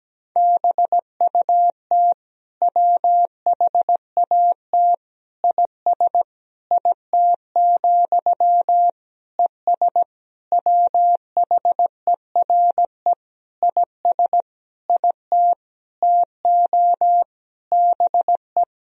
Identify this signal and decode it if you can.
Morse code, heard as a keyed tone, about 17 wpm, 700 Hz. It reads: BUT WHAT IS IT, ES WHERE IS IT TO BE
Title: Sentences from Top 300 Words and Abbreviations 17wpm